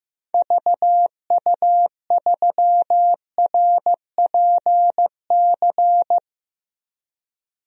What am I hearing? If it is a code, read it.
Morse code, heard as a keyed tone, about 15 wpm, 700 Hz. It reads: VU3RPC
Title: DXCC-100 Call Signs - 6 Characters - Repeated 2x - 15wpm